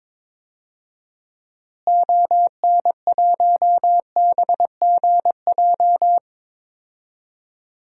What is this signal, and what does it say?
Morse code, heard as a keyed tone, about 22 wpm, 700 Hz. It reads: ON1BGJ
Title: DXCC-100 Call Signs - 6 Characters - Repeated 2x - 22wpm